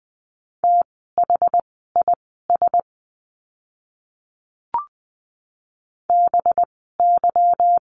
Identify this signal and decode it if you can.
Morse code, heard as a keyed tone, about 20 wpm, spaced as for 15 wpm, 700 Hz. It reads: THIS BY
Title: 185 Lesson 30f - Letter b - Words 20wpm